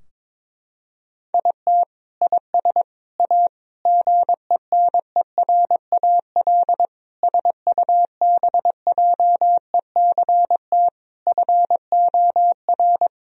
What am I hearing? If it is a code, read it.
Morse code, heard as a keyed tone, about 21 wpm, 700 Hz. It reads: IT IS A GENERAL SUBJECT FOR
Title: Sentences from Top 1000 Words - Review Part 3 - 22wpm